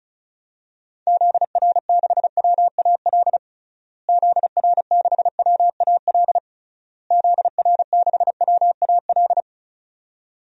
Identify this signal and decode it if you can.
Morse code, heard as a keyed tone, about 34 wpm, 700 Hz. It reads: ZR6WAL ZR6WAL ZR6WAL
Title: DXCC-100 Call Signs - 6 Characters - Repeated 3x - 35wpm